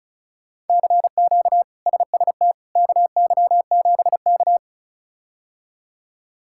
Morse code, heard as a keyed tone, about 36 wpm, 700 Hz. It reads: CQ SST KY7K
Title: K1USN SST QSO Practice - 35wpm